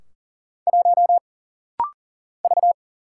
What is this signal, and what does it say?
Morse code, heard as a keyed tone, about 40 wpm, 700 Hz. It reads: 1 V
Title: Single Letter-Number - Rapid-Fire 40wpm